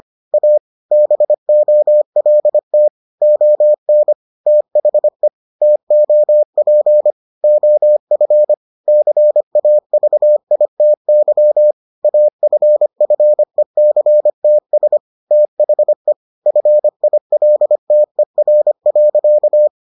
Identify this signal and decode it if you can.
Morse code, heard as a keyed tone, about 25 wpm, 600 Hz. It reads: A BOLT ON THE TOP OF CAVITY AFFECTS THE FILTER.